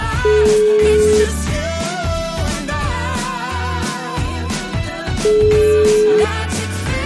Ringback tone (Europe, 425 Hz), with noise.